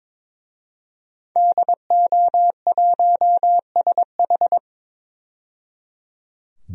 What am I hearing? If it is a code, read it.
Morse code, heard as a keyed tone, about 22 wpm, 700 Hz. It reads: DO1SH
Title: DXCC-100 Call Signs - 5 Characters - 22wpm